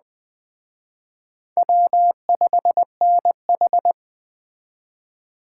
Morse code, heard as a keyed tone, about 20 wpm, 700 Hz. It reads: W5NH
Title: Speed Racing - USA 1x2 Call Signs 17wpm